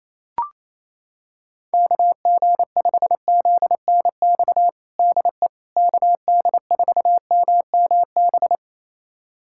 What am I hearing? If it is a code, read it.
Morse code, heard as a keyed tone, about 28 wpm, 700 Hz. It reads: KG5ZNX DE KD4MMB
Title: USA Callsign DE Callsign 28wpm